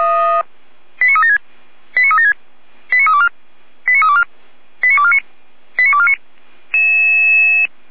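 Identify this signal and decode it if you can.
Five-tone selective call (ZVEI), heard as quick runs of five tones, tones of about 70 ms. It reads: 89376 89376 89324 89324 89249 89249
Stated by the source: Five-tone ZVEI selective call received on SDR.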